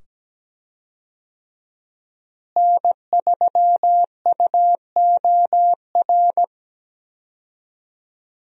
Morse code, heard as a keyed tone, about 17 wpm, 700 Hz. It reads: N3UOR
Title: USA Call Signs - 5 Characters - Repeated 2x - 17wpm